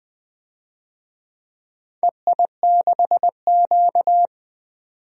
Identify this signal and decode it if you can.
Morse code, heard as a keyed tone, about 20 wpm, 700 Hz. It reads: EI6Q
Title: DXCC-100 Call Signs - 4 Characters - 20wpm